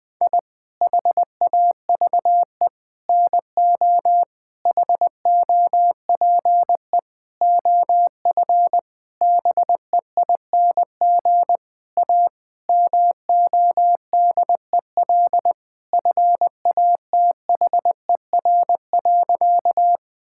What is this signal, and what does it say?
Morse code, heard as a keyed tone, about 21 wpm, 700 Hz. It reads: I HAVE NO HOPE OF BEING A MODEL FATHER.
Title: Sentences from Top 1000 Words - Review Part 5 - 20wpm